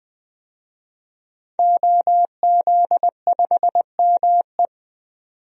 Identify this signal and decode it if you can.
Morse code, heard as a keyed tone, about 20 wpm, 700 Hz. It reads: OZ5ME
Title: DXCC-100 Call Signs - 5 Characters - 20wpm